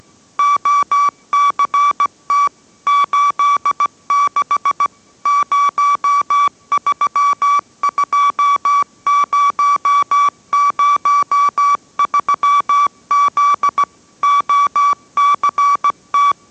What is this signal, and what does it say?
Morse code, heard as a keyed tone, about 17 wpm, 1150 Hz. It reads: OCT 86 032003Z OCT